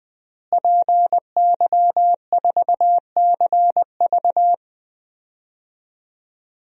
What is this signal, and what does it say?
Morse code, heard as a keyed tone, about 20 wpm, 700 Hz. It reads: PY4CV
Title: DXCC-100 Call Signs - 5 Characters - Repeated 2x - 20wpm